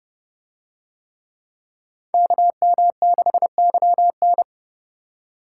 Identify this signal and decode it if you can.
Morse code, heard as a keyed tone, about 29 wpm, 700 Hz. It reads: KM6YN